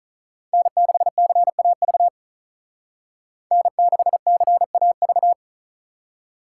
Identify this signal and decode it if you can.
Morse code, heard as a keyed tone, about 42 wpm, 700 Hz. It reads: N6CAV N6CAV
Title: Speed Racing - USA 1x3 Call Signs 35wpm